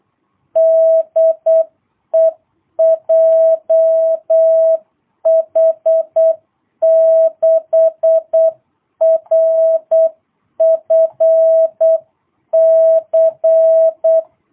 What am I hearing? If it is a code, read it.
Morse code, heard as a keyed tone, about 8 wpm, 660 Hz. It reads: DE JH6RFC
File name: jh8cbh / 2022cw